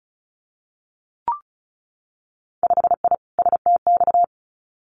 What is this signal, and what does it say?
Morse code, heard as a keyed tone, about 36 wpm, 700 Hz. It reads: 5I STX
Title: Winter Field Day Exchanges Encore 35wpm